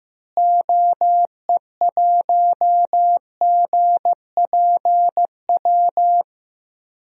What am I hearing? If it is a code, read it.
Morse code, heard as a keyed tone, about 15 wpm, 700 Hz. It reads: OE1GPW
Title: DXCC-100 Call Signs - 6 Characters - Repeated 3x - 15wpm